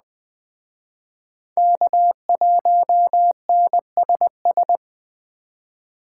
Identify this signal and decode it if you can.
Morse code, heard as a keyed tone, about 21 wpm, 700 Hz. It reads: K1NSS
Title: Speed Racing - USA 1x3 Call Signs 17wpm